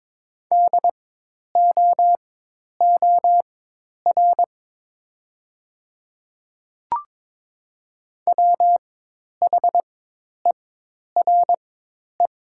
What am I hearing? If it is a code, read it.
Morse code, heard as a keyed tone, about 22 wpm, spaced as for 10 wpm, 700 Hz. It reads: DOOR WHERE
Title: 4x Character Spacing - Top 300 Words - 22wpm